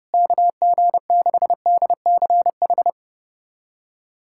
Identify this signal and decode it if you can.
Morse code, heard as a keyed tone, about 30 wpm, 700 Hz. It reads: KG6DCH
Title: USA Callsign DE Callsign 30wpm